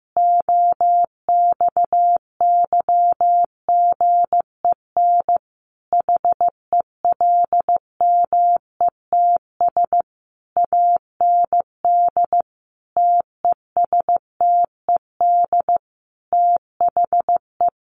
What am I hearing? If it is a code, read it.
Morse code, heard as a keyed tone, about 15 wpm, 700 Hz. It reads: OXYGEN HELMETS AND TESTED THE
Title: A Sound Of Thunder 15wpm